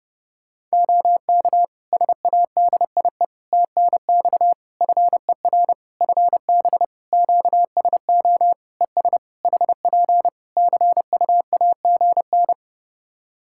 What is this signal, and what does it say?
Morse code, heard as a keyed tone, about 32 wpm, 700 Hz. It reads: OK SADIE TNX FER FB QSO ES HP CU AGN
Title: Practice QSO - Exchange 4 - Ending - 30wpm